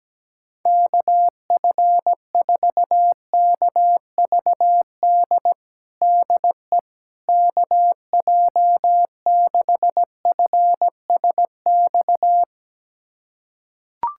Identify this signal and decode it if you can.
Morse code, heard as a keyed tone, about 17 wpm, 700 Hz. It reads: KF4KVD DE KJ6FSX
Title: USA Callsign DE Callsign - 6 Characters - 17wpm